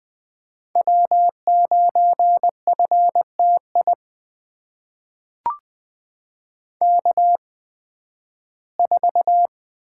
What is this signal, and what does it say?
Morse code, heard as a keyed tone, about 20 wpm, spaced as for 13 wpm, 700 Hz. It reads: W9FTI K4
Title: USA Call Signs - 5 Characters - 8x Character Spacing - 20wpm